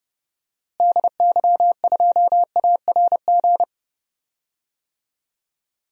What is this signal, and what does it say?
Morse code, heard as a keyed tone, about 30 wpm, 700 Hz. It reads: DY2ARG
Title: DXCC-100 Call Signs - 6 Characters - Repeated 3x - 30wpm